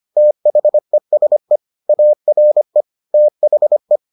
Morse code, heard as a keyed tone, about 25 wpm, 600 Hz. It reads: THESE ARE THE